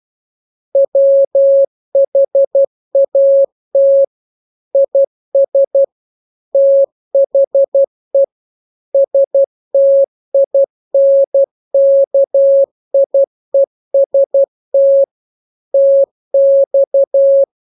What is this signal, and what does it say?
Morse code, heard as a keyed tone, about 12 wpm, 550 Hz. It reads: WHAT IS THE STINKIEST TX
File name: jokes / joke 27 12 0000